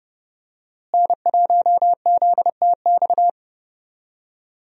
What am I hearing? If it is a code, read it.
Morse code, heard as a keyed tone, about 31 wpm, 700 Hz. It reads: N1ZTX
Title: USA Call Signs - 5 Characters - Repeated 3x - 30wpm